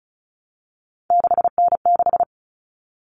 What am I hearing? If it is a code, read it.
Morse code, heard as a keyed tone, about 34 wpm, 700 Hz. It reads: 6N6
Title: All Cut Numbers - 3-Digit 35wpm